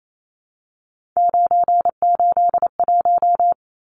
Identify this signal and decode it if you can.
Morse code, heard as a keyed tone, about 28 wpm, 700 Hz. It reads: 981
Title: Numbers - 3-Digit 28wpm